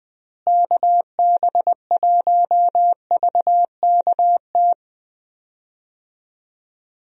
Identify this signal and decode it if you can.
Morse code, heard as a keyed tone, about 21 wpm, 700 Hz. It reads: KB1VKT
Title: USA Call Signs - 6 Characters - Repeated 3x - 20wpm